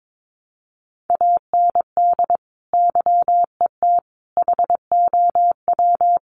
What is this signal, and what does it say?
Morse code, heard as a keyed tone, about 22 wpm, 700 Hz. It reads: AND YET HOW
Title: Sentences from Top 1300 Words 22wpm